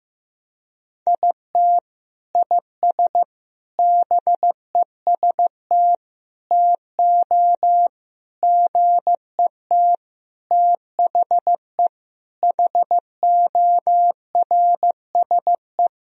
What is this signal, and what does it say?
Morse code, heard as a keyed tone, about 15 wpm, 700 Hz. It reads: IT IS BEST TO GET THE HORSE
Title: Sentences from Top 1000 Words - Review Part 1 - 15wpm